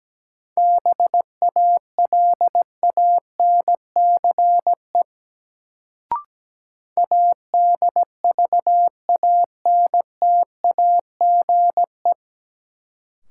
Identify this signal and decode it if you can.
Morse code, heard as a keyed tone, about 17 wpm, 700 Hz. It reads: BALANCE ADVANTAGE
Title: Sentences from Top 900 Words - New Word Drill - 17wpm